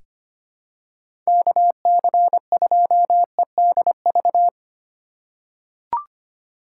Morse code, heard as a keyed tone, about 25 wpm, 700 Hz. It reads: KC2EDV
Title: USA Call Signs - 4x Character Spacing - 25wpm